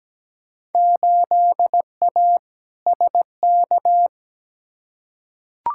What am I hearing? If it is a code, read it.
Morse code, heard as a keyed tone, about 17 wpm, 700 Hz. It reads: 8A SK
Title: Field Day Exchanges Encore - 17wpm